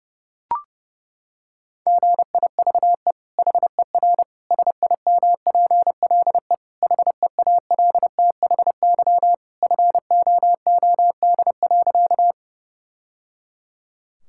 Morse code, heard as a keyed tone, about 32 wpm, 700 Hz. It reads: GIVE HER SIMPLE HEALTHY FOOD.
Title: Sentences from Top 1000 Words - Review Part 5 - 30wpm